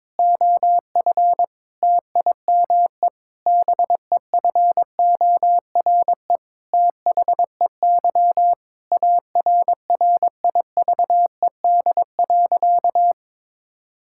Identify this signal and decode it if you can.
Morse code, heard as a keyed tone, about 22 wpm, 700 Hz. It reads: OF TIME BEFORE THEY ARRIVED.
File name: practice / 22wpm / Sentences with 7-Word Phrases - 22wpm